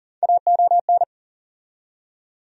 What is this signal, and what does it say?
Morse code, heard as a keyed tone, about 39 wpm, 700 Hz. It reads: AON
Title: USA Call Sign Postfixes - 3 Letters 40wpm